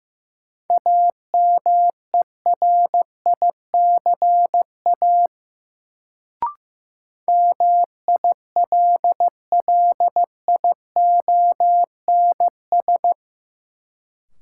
Morse code, heard as a keyed tone, about 15 wpm, 700 Hz. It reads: AMERICA MILLIONS
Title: Sets of 5 Words - New Word Drill - 15wpm